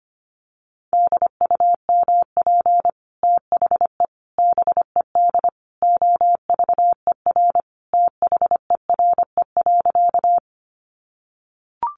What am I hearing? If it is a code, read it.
Morse code, heard as a keyed tone, about 25 wpm, 700 Hz. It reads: DUMP THE BED OVER THERE.
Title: Sentences from Top 1300 Words 25wpm